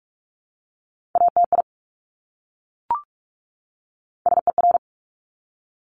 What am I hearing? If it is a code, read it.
Morse code, heard as a keyed tone, about 46 wpm, 690 Hz. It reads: ATI SER
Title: Top Three Letter Combinations in Words 45wpm